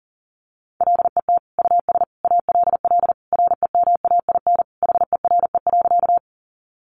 Morse code, heard as a keyed tone, about 40 wpm, 700 Hz. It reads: LET US ALL REMAIN HERE.
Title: Sentences from Top 1200 Words 40wpm